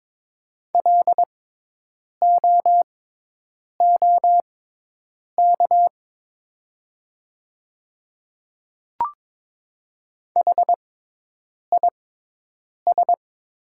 Morse code, heard as a keyed tone, about 22 wpm, spaced as for 8 wpm, 700 Hz. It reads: LOOK HIS
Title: 6x Character Spacing - Top 200 Words - 22wpm